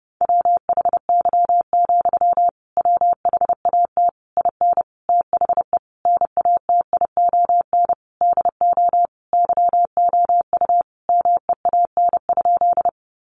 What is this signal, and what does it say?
Morse code, heard as a keyed tone, about 30 wpm, 700 Hz. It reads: WHY, WHAT IN THE NATION DO YOU MEAN?
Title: Sentences from Top 900 Words 30wpm